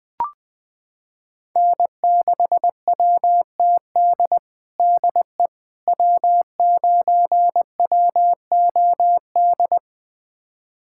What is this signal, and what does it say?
Morse code, heard as a keyed tone, about 20 wpm, 700 Hz. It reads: N6WTD DE W9WOD
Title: USA Callsign DE Callsign - 5 Characters - 20wpm